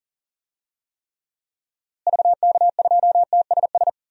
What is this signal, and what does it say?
Morse code, heard as a keyed tone, about 38 wpm, 700 Hz. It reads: VK2TSS
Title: DXCC-100 Call Signs - 6 Characters - Repeated 2x - 40wpm